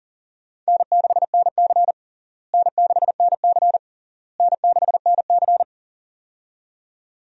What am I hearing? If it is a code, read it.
Morse code, heard as a keyed tone, about 40 wpm, 700 Hz. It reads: N6NC N6NC N6NC
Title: USA Call Signs - 4 Characters - Repeated 3x - 40wpm